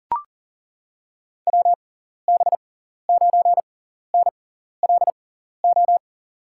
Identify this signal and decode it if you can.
Morse code, heard as a keyed tone, about 40 wpm, spaced as for 14 wpm, 700 Hz. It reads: WB9NLO
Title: USA Call Signs - 6 Characters - 6x Character Spacing - 40wpm